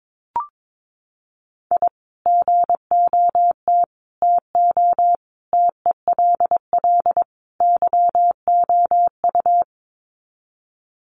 Morse code, heard as a keyed tone, about 22 wpm, 700 Hz. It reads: I GOT TO TELL YOU
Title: Sets of 5 Words Encore2 22wpm